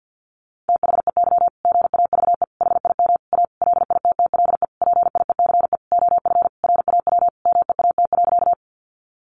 Morse code, heard as a keyed tone, about 50 wpm, 700 Hz. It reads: THEY GAVE HIM A LITTLE PIECE OF RAW MEAT.
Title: Sentences from Top 1500 Words 50wpm